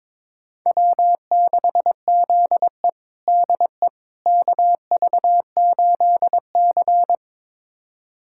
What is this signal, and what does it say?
Morse code, heard as a keyed tone, about 22 wpm, 700 Hz. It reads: W6ZE DE KV8C
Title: USA Callsign DE Callsign - 4 Characters - 22wpm